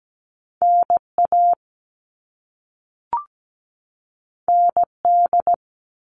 Morse code, heard as a keyed tone, about 17 wpm, 700 Hz. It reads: NA ND
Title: All Cut Numbers - 2-Digit 17wpm